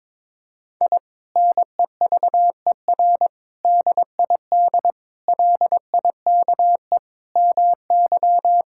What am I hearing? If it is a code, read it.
Morse code, heard as a keyed tone, about 22 wpm, 700 Hz. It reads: I NEVER DID LIKE MY
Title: Sentences from Top 1000 Words - Review Part 5 - 22wpm